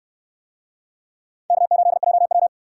Morse code, heard as a keyed tone, about 64 wpm, 700 Hz. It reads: K8JW